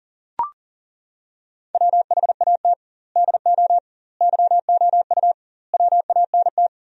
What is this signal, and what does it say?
Morse code, heard as a keyed tone, about 40 wpm, 700 Hz. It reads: WHAT DO YOU WANT
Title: 2x Word-Spacing - Sentences from Top 200 Words - 40wpm